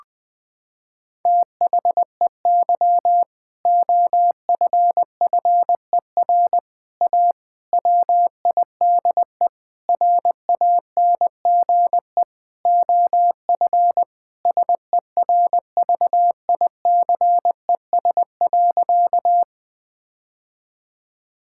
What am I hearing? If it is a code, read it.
Morse code, heard as a keyed tone, about 20 wpm, 700 Hz. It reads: THEY OFFER A WIDE RANGE OF SERVICES.
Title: Sentences with 4-Word Phrases - 20wpm